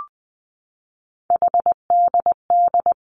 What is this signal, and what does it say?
Morse code, heard as a keyed tone, about 20 wpm, 700 Hz. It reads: HDD
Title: Three Challenging Letter-Number Combinations 20wpm